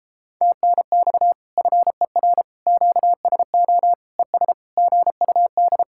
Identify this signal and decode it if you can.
Morse code, heard as a keyed tone, about 33 wpm, 700 Hz. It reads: TNX FER QSO ES GUD